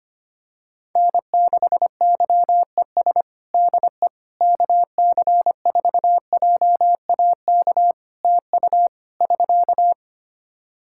Morse code, heard as a keyed tone, about 25 wpm, 700 Hz. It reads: N6YES DE KC4JAK TU <SK>
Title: K1USN SST QSO Practice 25wpm